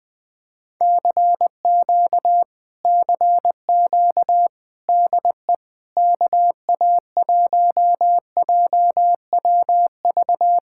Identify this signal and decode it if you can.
Morse code, heard as a keyed tone, about 19 wpm, 700 Hz. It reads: CQ CQ DE KA1JWV
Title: Speed-Racing Common QSO Phrases - 20wpm